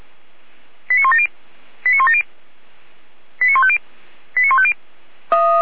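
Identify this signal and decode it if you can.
Five-tone selective call (ZVEI), heard as quick runs of five tones, tones of about 80 ms. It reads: 89170 89170 89150 89150
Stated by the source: ZVEI five-tone selective call received by SDR.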